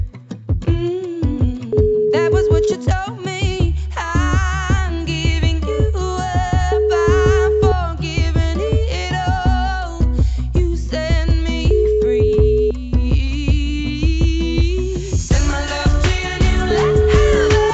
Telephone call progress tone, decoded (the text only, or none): ringback tone